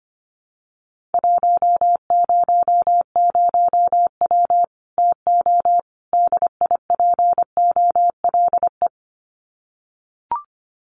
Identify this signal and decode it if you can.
Morse code, heard as a keyed tone, about 25 wpm, 700 Hz. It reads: 100W TO DIPOLE
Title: Common QSO Phrases 25wpm